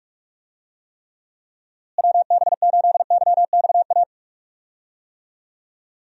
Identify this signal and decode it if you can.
Morse code, heard as a keyed tone, about 44 wpm, 700 Hz. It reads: WB8YXA
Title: USA Call Signs - 6 Characters - Repeated 2x - 45wpm